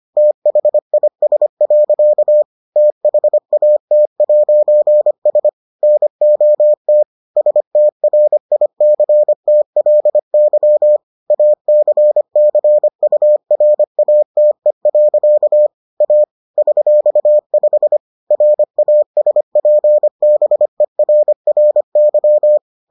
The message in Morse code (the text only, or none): THIS. THAT'S NOT STRICTLY ACCURATE. A $5 RASPBERRY